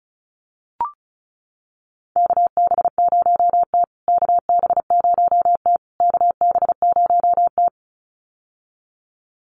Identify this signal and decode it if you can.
Morse code, heard as a keyed tone, about 34 wpm, 700 Hz. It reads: KB0T KB0T KB0T
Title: USA Call Signs Repeated 3x 35wpm